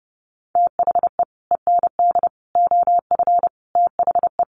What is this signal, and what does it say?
Morse code, heard as a keyed tone, about 30 wpm, 700 Hz. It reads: THE END OF THE
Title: Sets of 5 Words 30wpm